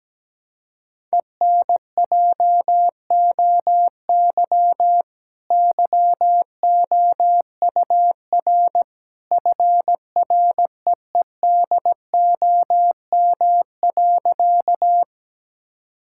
Morse code, heard as a keyed tone, about 17 wpm, 700 Hz. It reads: ENJOY YOUR FREEDOM.
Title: Sentences from Top 1000 Words - Review Part 5 - 17wpm